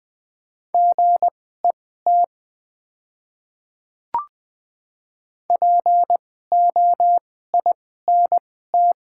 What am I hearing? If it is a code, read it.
Morse code, heard as a keyed tone, about 20 wpm, spaced as for 14 wpm, 700 Hz. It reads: GET POINT
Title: 2x Character Spacing - Top 200 Words - 20wpm